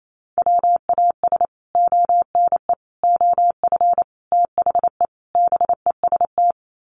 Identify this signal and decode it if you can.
Morse code, heard as a keyed tone, about 28 wpm, 700 Hz. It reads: WAS ONE OF THE BEST
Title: Sets of 5 Words Encore2 28wpm